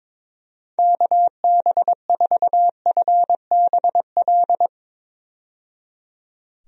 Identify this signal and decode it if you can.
Morse code, heard as a keyed tone, about 22 wpm, 700 Hz. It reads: KB4FBL
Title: USA Call Signs - 6 Characters - 22wpm